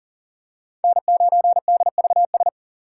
Morse code, heard as a keyed tone, about 40 wpm, 700 Hz. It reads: N9DVS
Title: USA Call Signs - 5 Characters - 2x Character Spacing - 40wpm